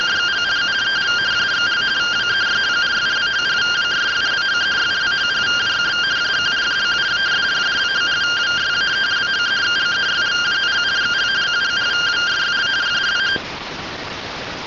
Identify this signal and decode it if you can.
Two-tone teleprinter signal, about 170 Hz shift.